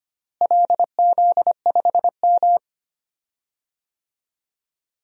Morse code, heard as a keyed tone, about 25 wpm, 700 Hz. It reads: LZ5M
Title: DXCC-100 Call Signs - 4 Characters - Repeated 3x - 25wpm